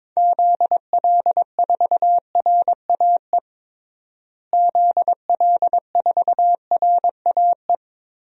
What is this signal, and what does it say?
Morse code, heard as a keyed tone, about 22 wpm, 700 Hz. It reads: ZL4RAE ZL4RAE
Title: DXCC-100 Call Signs - 6 Characters - Repeated 3x - 22wpm